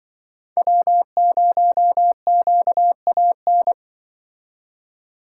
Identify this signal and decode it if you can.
Morse code, heard as a keyed tone, about 23 wpm, 700 Hz. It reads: W0QAN
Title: Speed Racing - USA 1x3 Call Signs 20wpm